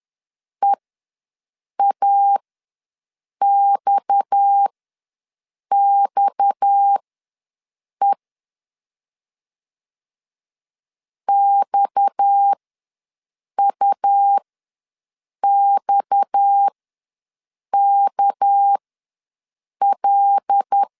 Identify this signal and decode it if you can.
Morse code, heard as a keyed tone, about 10 wpm, spaced as for 6 wpm, 800 Hz. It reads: EAXXE XUXKL
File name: LEZIONI / 6 PKX0 / 6 PKX ws 10wpm 3706